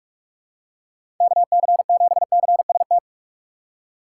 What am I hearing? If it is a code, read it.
Morse code, heard as a keyed tone, about 46 wpm, 700 Hz. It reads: KC7CST
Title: USA Call Signs - 6 Characters - 4x Character Spacing - 45wpm